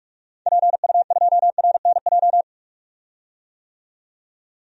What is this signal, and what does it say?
Morse code, heard as a keyed tone, about 41 wpm, 700 Hz. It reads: PU2RNJ
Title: DXCC-100 Call Signs - 6 Characters - 45wpm